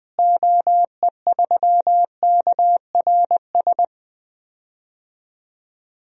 Morse code, heard as a keyed tone, about 20 wpm, 700 Hz. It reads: OE3KRS
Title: DXCC-100 Call Signs - 6 Characters - Repeated 2x - 20wpm